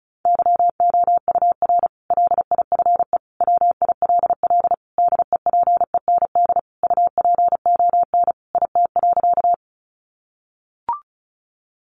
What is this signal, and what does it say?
Morse code, heard as a keyed tone, about 34 wpm, 700 Hz. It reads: YOUR LIFE WILL DEPEND UPON IT.
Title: Sentences from Top 1300 Words 35wpm